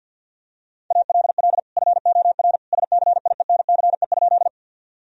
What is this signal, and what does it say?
Morse code, heard as a keyed tone, about 50 wpm, 700 Hz. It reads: ALL FOR SCIENCE?
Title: Sentences from Top 1000 Words - Review Part 2 - 50wpm